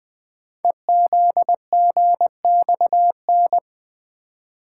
Morse code, heard as a keyed tone, about 20 wpm, 700 Hz. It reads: EZGXN
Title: Code Groups - 5 Letter:Numbers - 20wpm